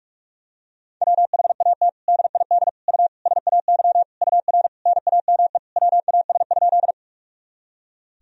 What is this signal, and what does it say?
Morse code, heard as a keyed tone, about 46 wpm, 700 Hz. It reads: WHAT DID U SAY UR NAME WAS?
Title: Sentences from Top 200 Words and Abbreviations 45wpm